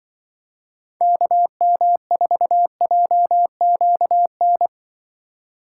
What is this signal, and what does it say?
Morse code, heard as a keyed tone, about 24 wpm, 700 Hz. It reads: KM4JQN
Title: Speed Racing - USA 2x3 Call Signs 20wpm